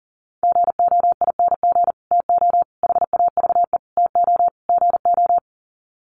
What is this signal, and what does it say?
Morse code, heard as a keyed tone, about 40 wpm, 700 Hz. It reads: GOING TO HAVE TO GO
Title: Sets of 5 Words Encore 40wpm